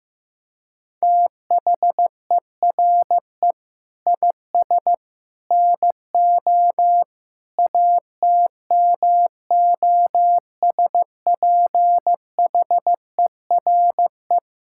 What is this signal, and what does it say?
Morse code, heard as a keyed tone, about 15 wpm, 700 Hz. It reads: THERE IS NO ATMOSPHERE
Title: Sentences from Top 1600 Words 15wpm